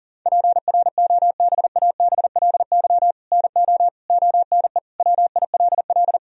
Morse code, heard as a keyed tone, about 37 wpm, 700 Hz. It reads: PROBABLY NO ONE WILL
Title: Sentences from Top 1000 Words - Review Part 4 - 40wpm